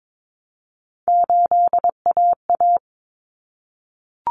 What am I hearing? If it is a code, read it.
Morse code, heard as a keyed tone, about 22 wpm, 700 Hz. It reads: 8AA
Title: Common Cut Numbers - 3-Digit 22wpm